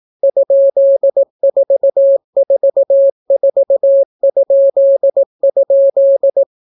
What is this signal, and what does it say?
Morse code, heard as a keyed tone, about 18 wpm, 550 Hz. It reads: ?444??